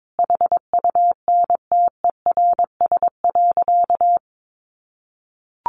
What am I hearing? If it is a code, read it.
Morse code, heard as a keyed tone, about 22 wpm, 700 Hz. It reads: HUNTERS.
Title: A Sound Of Thunder 22wpm